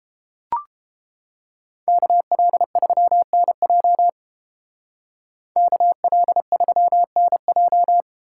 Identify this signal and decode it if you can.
Morse code, heard as a keyed tone, about 28 wpm, 700 Hz. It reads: KL3NJ KL3NJ